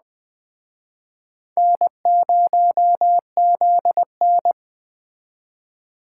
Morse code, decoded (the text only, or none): N0ZN